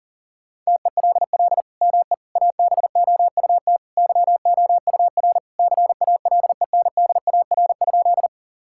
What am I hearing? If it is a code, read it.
Morse code, heard as a keyed tone, about 40 wpm, 700 Hz. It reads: TELL ME ABOUT YOUR CALENDAR?
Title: Sentences from Top 1600 Words 40wpm